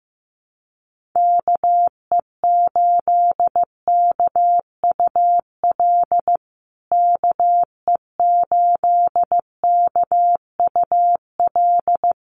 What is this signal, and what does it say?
Morse code, heard as a keyed tone, about 15 wpm, 700 Hz. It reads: KE8KUL KE8KUL
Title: USA Call Signs Repeated 3x 15wpm